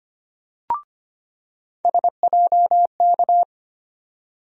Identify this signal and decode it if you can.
Morse code, heard as a keyed tone, about 25 wpm, 700 Hz. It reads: SJK
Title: USA Call Sign Postfixes - 3 Letters 25wpm